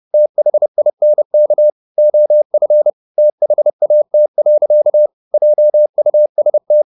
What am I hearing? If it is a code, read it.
Morse code, heard as a keyed tone, about 30 wpm, 600 Hz. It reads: THINK OF THAT. JUST